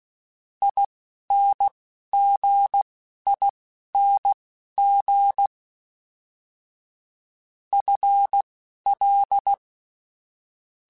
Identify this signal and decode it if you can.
Morse code, heard as a keyed tone, about 16 wpm, spaced as for 11 wpm, 800 Hz. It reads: INGING FL
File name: Special INGs / Special INGs-mp3s / Dble Ings sp @10